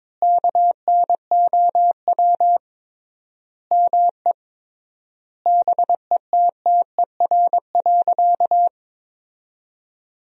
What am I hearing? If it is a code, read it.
Morse code, heard as a keyed tone, about 23 wpm, 700 Hz. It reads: KNOW ME BETTER.
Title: 3x Word-Spacing - Sentences from Top 400 Words - 22wpm